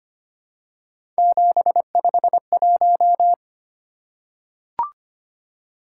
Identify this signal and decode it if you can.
Morse code, heard as a keyed tone, about 25 wpm, 700 Hz. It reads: 751
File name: Three Letter-Number Combinations - 25wpm